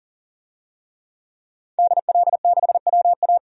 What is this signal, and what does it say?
Morse code, heard as a keyed tone, about 40 wpm, 700 Hz. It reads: DL6WA
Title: DXCC-100 Call Signs - 5 Characters - 40wpm